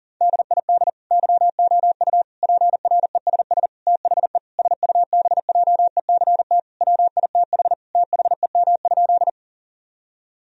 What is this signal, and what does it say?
Morse code, heard as a keyed tone, about 37 wpm, 700 Hz. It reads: DID YOU PRESS THE SUBJECT WITH THEM?
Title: Sentences from Top 1000 Words - Review Part 1 - 40wpm